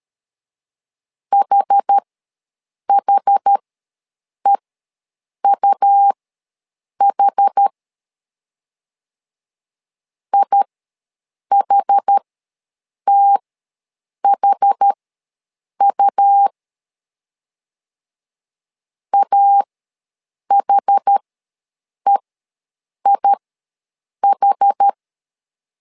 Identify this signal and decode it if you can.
Morse code, heard as a keyed tone, about 13 wpm, spaced as for 7 wpm, 800 Hz. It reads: HHEUH IHTHU AHEIH
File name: LEZIONI / 2 UHDN / 2 UH ws 12wpm 1730